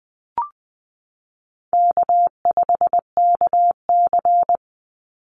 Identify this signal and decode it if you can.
Morse code, heard as a keyed tone, about 20 wpm, 700 Hz. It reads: K5KC
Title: Easy Call Signs Repeated 3x - 20wpm